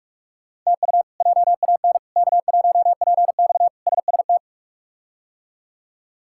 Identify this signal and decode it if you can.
Morse code, heard as a keyed tone, about 41 wpm, 700 Hz. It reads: TU JAN K1PX SST
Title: K1USN SST QSO Practice - 45wpm